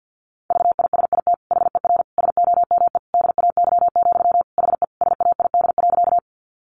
Morse code, heard as a keyed tone, about 50 wpm, 700 Hz. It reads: VISIT HER SOME DAY, HE SAID.
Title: Sentences from Top 900 Words 50wpm